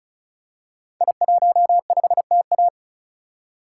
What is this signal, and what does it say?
Morse code, heard as a keyed tone, about 35 wpm, 700 Hz. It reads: I15TA
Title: Code Groups - 5 Letter:Numbers - 35wpm